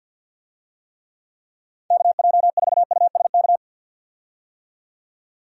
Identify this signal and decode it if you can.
Morse code, heard as a keyed tone, about 51 wpm, 700 Hz. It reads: KJ4USK
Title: USA Call Signs - 6 Characters - 50wpm